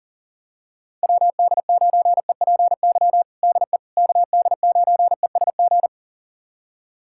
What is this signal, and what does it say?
Morse code, heard as a keyed tone, about 37 wpm, 700 Hz. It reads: WD9EPY DE KD9ESG
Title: USA Callsign DE Callsign 40wpm